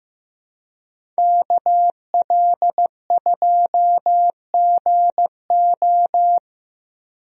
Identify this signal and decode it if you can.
Morse code, heard as a keyed tone, about 15 wpm, 700 Hz. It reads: KL2GO
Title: USA Call Signs - 5 Characters - Repeated 2x - 15wpm